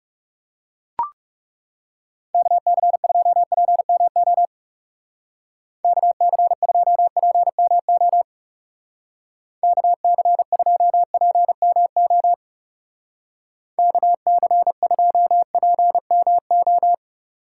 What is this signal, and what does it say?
Morse code, heard as a keyed tone, about 37 wpm, 700 Hz. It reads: KC2PMO KC2PMO KC2PMO KC2PMO
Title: Speed Racing - USA 2x3 Call Signs 30wpm